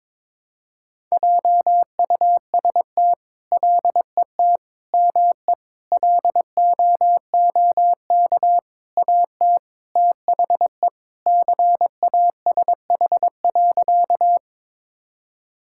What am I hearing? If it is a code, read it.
Morse code, heard as a keyed tone, about 21 wpm, 700 Hz. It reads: JUST LET ME LOOK AT THE CASH.
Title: Sentences from Top 1000 Words - Review Part 5 - 22wpm